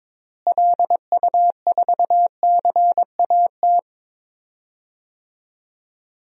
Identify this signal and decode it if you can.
Morse code, heard as a keyed tone, about 22 wpm, 700 Hz. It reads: LU4CAT
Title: DXCC-100 Call Signs - 6 Characters - Repeated 2x - 22wpm